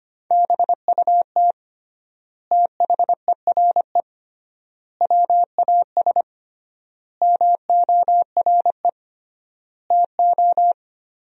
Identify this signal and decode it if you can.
Morse code, heard as a keyed tone, about 25 wpm, 700 Hz. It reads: BUT THERE WAS MORE TO
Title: 3x Word-Spacing - Sentences from Top 100 Words - 25wpm